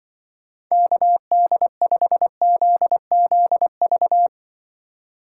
Morse code, heard as a keyed tone, about 24 wpm, 700 Hz. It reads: KD5ZZV
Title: Speed Racing - USA 2x3 Call Signs 20wpm